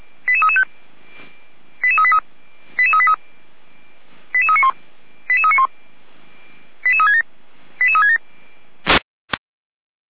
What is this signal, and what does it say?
Five-tone selective call (ZVEI), heard as quick runs of five tones, tones of about 70 ms. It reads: 88335 88383 88383 88391 88391 88367 88367
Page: SDR/ZVEI